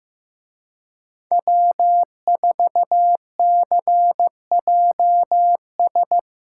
Morse code, heard as a keyed tone, about 18 wpm, 700 Hz. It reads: W4CJS
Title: Speed Racing - USA 1x3 Call Signs 15wpm